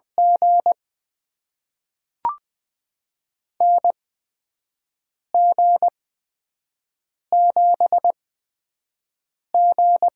Morse code, heard as a keyed tone, about 20 wpm, spaced as for 5 wpm, 700 Hz. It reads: G NG7G
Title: USA Call Signs - 4 Characters - 8x Character Spacing - 20wpm